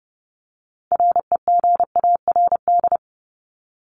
Morse code, heard as a keyed tone, about 30 wpm, 700 Hz. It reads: REGARD
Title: ING Suffix - Top 100 Words with Base Words 30wpm